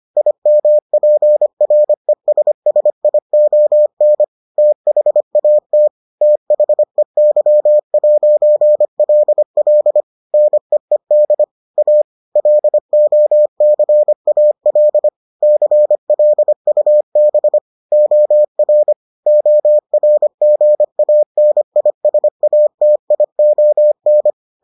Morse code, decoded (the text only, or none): IMPRESSION THAT THEY'LL NEED A LOCAL CLUB OR ORGANISATION